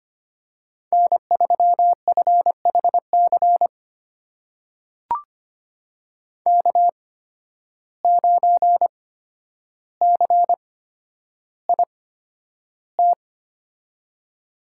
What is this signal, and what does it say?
Morse code, heard as a keyed tone, about 25 wpm, spaced as for 11 wpm, 700 Hz. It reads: N3FHC K9CIT
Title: USA Call Signs - 5 Characters - 8x Character Spacing - 25wpm